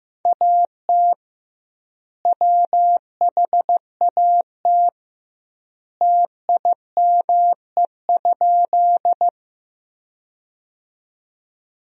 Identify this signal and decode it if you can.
Morse code, heard as a keyed tone, about 16 wpm, 700 Hz. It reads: AT WHAT TIME?
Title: 2x Word-Spacing - Sentences from Top 100 Words - 15wpm